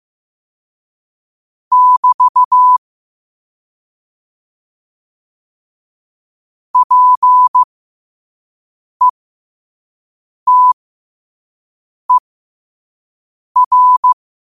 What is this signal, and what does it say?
Morse code, heard as a keyed tone, about 15 wpm, spaced as for 5 wpm, 1000 Hz. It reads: <BT> PETER
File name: qotdfiles / 5wpm